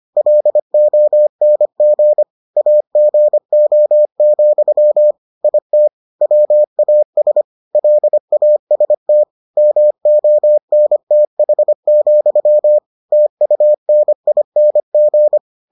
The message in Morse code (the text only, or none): LONG AGO, IT WAS LAST MONTH, TUNING